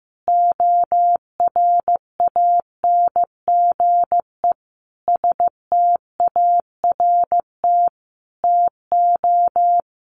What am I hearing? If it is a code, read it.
Morse code, heard as a keyed tone, about 15 wpm, 700 Hz. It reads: ORANGE START TO
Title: Easy Phrases 15wpm